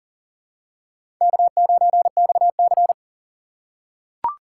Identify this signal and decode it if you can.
Morse code, heard as a keyed tone, about 40 wpm, 700 Hz. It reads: K9XC
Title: USA Call Signs - 4 Characters - 3x Character Spacing - 40wpm